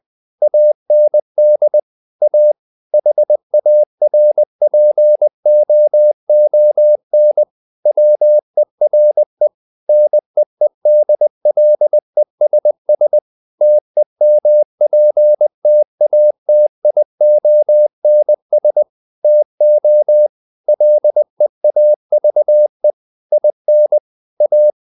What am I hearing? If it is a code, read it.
Morse code, heard as a keyed tone, about 20 wpm, 600 Hz. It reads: AND A HARPOON WERE NEEDLESS TEMPTATIONS TO LEAVE IN A
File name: the old man and the sea / the old man and the sea part 1 20wpm